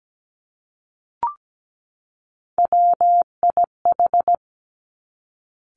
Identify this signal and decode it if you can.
Morse code, heard as a keyed tone, about 17 wpm, 700 Hz. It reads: WIH
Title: Three Challenging Letter-Number Combinations 17wpm